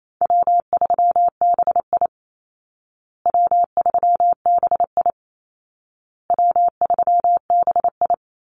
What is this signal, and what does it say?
Morse code, heard as a keyed tone, about 28 wpm, 700 Hz. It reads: W3BI W3BI W3BI
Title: Call Signs Repeated 2x - 28wpm